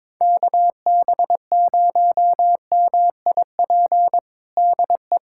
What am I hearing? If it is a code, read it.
Morse code, heard as a keyed tone, about 21 wpm, 700 Hz. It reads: KB0MIP DE
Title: USA Callsign DE Callsign 22wpm